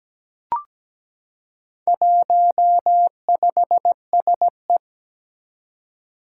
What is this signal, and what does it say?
Morse code, heard as a keyed tone, about 17 wpm, 700 Hz. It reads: 15SE
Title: Four Challenging Letter-Number Combinations 17wpm